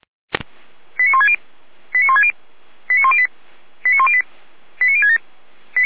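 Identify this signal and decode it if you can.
Five-tone selective call (ZVEI), heard as quick runs of five tones, tones of about 70 ms. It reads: 89160 89160 89108 89108 89067
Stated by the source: Received on SDR, ZVEI code 89150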